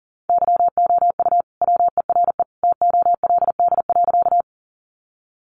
Morse code, heard as a keyed tone, about 40 wpm, 700 Hz. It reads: YOU WERE TOLD.
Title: Sentences from Top 400 Words 40wpm